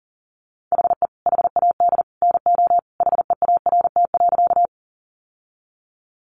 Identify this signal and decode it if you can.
Morse code, heard as a keyed tone, about 40 wpm, 700 Hz. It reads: HE HAD NO HEART.
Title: Sentences from Top 1200 Words 40wpm